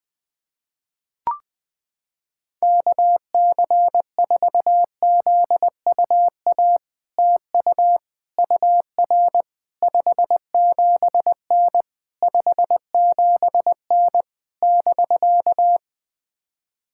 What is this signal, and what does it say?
Morse code, heard as a keyed tone, about 20 wpm, 700 Hz. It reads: KC4ZUA TU UR 57N 57N <BK>
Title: POTA QSO Practice - 20wpm